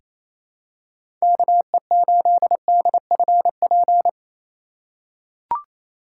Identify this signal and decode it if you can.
Morse code, heard as a keyed tone, about 28 wpm, 700 Hz. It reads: KE8DFP
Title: USA Call Signs - 6 Characters - 3x Character Spacing - 28wpm